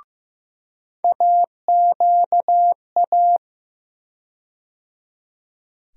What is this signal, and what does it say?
Morse code, heard as a keyed tone, about 15 wpm, 700 Hz. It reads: AQA
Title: USA Call Sign Postfixes - 3 Letters 15wpm